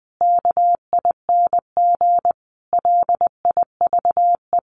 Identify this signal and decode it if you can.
Morse code, heard as a keyed tone, about 20 wpm, 700 Hz. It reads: KING LIVE
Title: Sets of 5 Words Encore2 20wpm